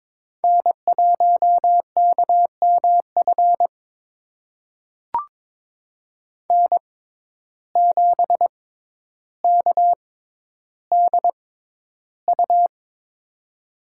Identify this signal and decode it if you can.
Morse code, heard as a keyed tone, about 21 wpm, spaced as for 11 wpm, 700 Hz. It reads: N1KMF N7KDU
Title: USA Call Signs - 5 Characters - 6x Character Spacing - 22wpm